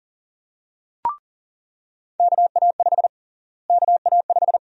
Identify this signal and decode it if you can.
Morse code, heard as a keyed tone, about 40 wpm, 700 Hz. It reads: KA5 KA5
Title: USA Call Sign Prefixes - Repeated 2x - 40wpm